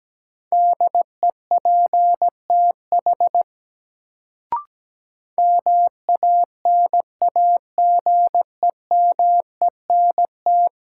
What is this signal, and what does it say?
Morse code, heard as a keyed tone, about 17 wpm, 700 Hz. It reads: DEPTH MANAGEMENT
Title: Sentences from Top 1200 Words - New Word Drill Alt - 17wpm